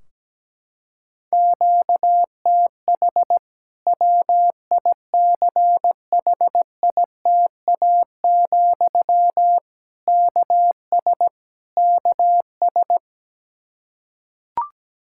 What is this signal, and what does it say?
Morse code, heard as a keyed tone, about 17 wpm, 700 Hz. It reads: QTH WICHITA, KS KS
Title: K1USN SST QSO Practice 17wpm